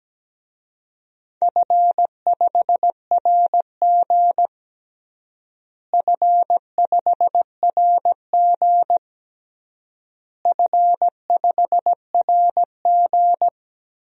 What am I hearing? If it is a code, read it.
Morse code, heard as a keyed tone, about 17 wpm, 700 Hz. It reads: F5RG F5RG F5RG
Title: DXCC-100 Call Signs - 4 Characters - Repeated 3x - 17wpm